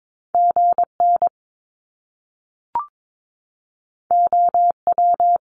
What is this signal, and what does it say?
Morse code, heard as a keyed tone, about 22 wpm, 700 Hz. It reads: GN OW
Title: Top Two Letter Combinations in Words 22wpm